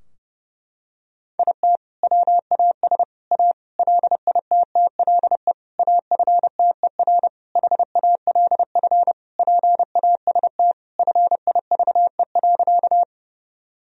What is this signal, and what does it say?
Morse code, heard as a keyed tone, about 30 wpm, 700 Hz. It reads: IT WAS A LITTLE AFTER HALF PAST FIVE.
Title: Sentences from Top 1000 Words - Review Part 3 - 30wpm